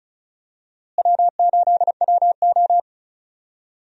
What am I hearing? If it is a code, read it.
Morse code, heard as a keyed tone, about 36 wpm, 700 Hz. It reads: W8WO
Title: USA Call Signs - 4 Characters - Repeated 2x - 35wpm